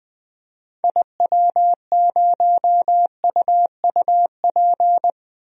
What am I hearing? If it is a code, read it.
Morse code, heard as a keyed tone, about 20 wpm, 700 Hz. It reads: IW0UUP
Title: DXCC-100 Call Signs - 6 Characters - 20wpm